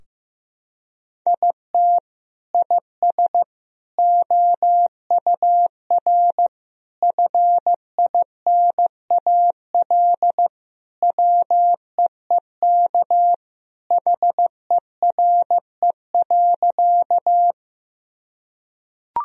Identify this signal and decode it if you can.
Morse code, heard as a keyed tone, about 16 wpm, 700 Hz. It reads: IT IS OUR FINAL WEEK HERE.
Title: Sentences from Top 1000 Words - Review Part 2 - 15wpm